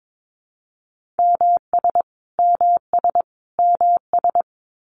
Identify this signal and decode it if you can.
Morse code, heard as a keyed tone, about 22 wpm, 700 Hz. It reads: MS MS MS
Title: US States Repeated 3x 22wpm